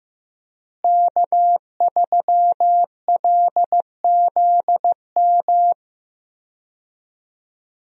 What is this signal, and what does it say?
Morse code, heard as a keyed tone, about 15 wpm, 700 Hz. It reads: K3LZM
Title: USA Call Signs - 5 Characters - Repeated 3x - 15wpm